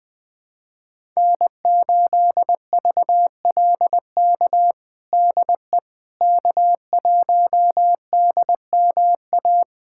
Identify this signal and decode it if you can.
Morse code, heard as a keyed tone, about 20 wpm, 700 Hz. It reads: N8VLK DE K1DMA
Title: USA Callsign DE Callsign - 5 Characters - 20wpm